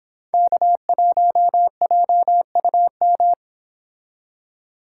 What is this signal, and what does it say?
Morse code, heard as a keyed tone, about 28 wpm, 700 Hz. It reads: K1JUM
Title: Speed Racing - USA 1x3 Call Signs 22wpm